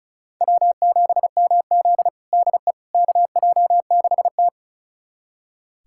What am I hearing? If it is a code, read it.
Morse code, heard as a keyed tone, about 36 wpm, 700 Hz. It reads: W7MZ DE KJ6T
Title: USA Callsign DE Callsign - 4 Characters - 35wpm